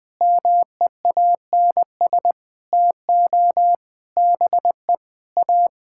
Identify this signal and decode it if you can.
Morse code, heard as a keyed tone, about 19 wpm, 700 Hz. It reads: MEANS TO BE A
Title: Sentences with 6-Word Phrases - 20wpm